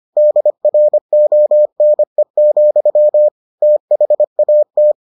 Morse code, heard as a keyed tone, about 25 wpm, 600 Hz. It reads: DRONE, THAT